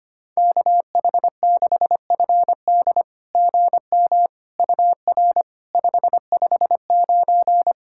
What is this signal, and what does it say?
Morse code, heard as a keyed tone, about 25 wpm, 700 Hz. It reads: KH6FD GM UR 559
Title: SOTA QSO Practice 25wpm